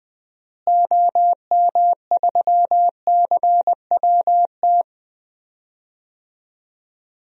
Morse code, heard as a keyed tone, about 20 wpm, 700 Hz. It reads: OM3CWT
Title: DXCC-100 Call Signs - 6 Characters - Repeated 2x - 20wpm